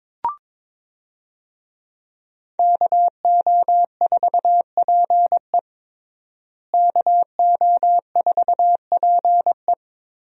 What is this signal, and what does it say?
Morse code, heard as a keyed tone, about 22 wpm, 700 Hz. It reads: KO4PE KO4PE
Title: USA Call Signs - 5 Characters - Repeated 2x - 22wpm